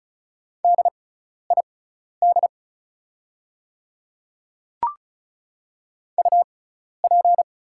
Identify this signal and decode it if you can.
Morse code, heard as a keyed tone, about 34 wpm, spaced as for 12 wpm, 700 Hz. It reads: DID UP
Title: 6x Character Spacing - Top 100 Words - 35wpm